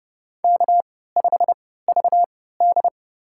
Morse code, heard as a keyed tone, about 30 wpm, spaced as for 17 wpm, 700 Hz. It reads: K5VD
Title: USA Call Signs - 4 Characters - 3x Character Spacing - 30wpm